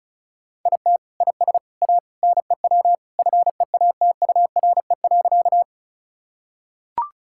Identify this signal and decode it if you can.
Morse code, heard as a keyed tone, about 34 wpm, 700 Hz. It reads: IT IS A NEW FEATURE.
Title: Sentences from Top 1000 Words - Review Part 5 - 35wpm